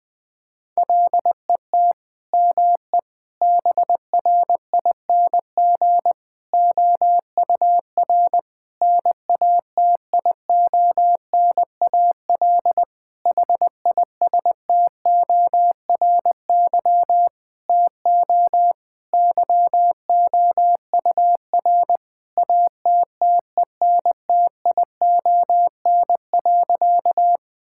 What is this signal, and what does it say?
Morse code, heard as a keyed tone, about 20 wpm, 700 Hz. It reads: LET ME BRING OUR NATIONAL HISTORY TO YOUR ATTENTION.
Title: Sentences from Top 1000 Words - Review Part 6 - 20wpm